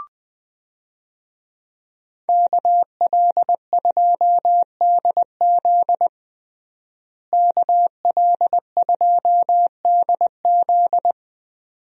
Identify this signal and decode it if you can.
Morse code, heard as a keyed tone, about 20 wpm, 700 Hz. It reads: KL2DZ KL2DZ
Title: USA Call Signs - 5 Characters - Repeated 2x - 20wpm